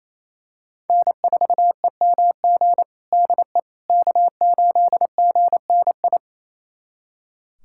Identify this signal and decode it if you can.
Morse code, heard as a keyed tone, about 28 wpm, 700 Hz. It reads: N4EMG DE K8GNI
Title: USA Callsign DE Callsign - 5 Characters - 28wpm